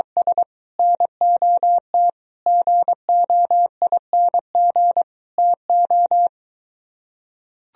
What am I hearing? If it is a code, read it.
Morse code, heard as a keyed tone, about 24 wpm, 700 Hz. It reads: S NOT GOING TO